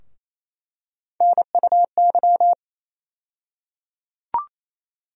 Morse code, heard as a keyed tone, about 28 wpm, 700 Hz. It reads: NUY
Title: USA Call Sign Postfixes - 3 Letters 28wpm